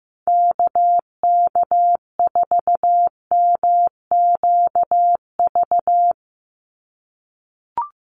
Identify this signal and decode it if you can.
Morse code, heard as a keyed tone, about 15 wpm, 700 Hz. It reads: KK4MQV
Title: USA Call Signs 15wpm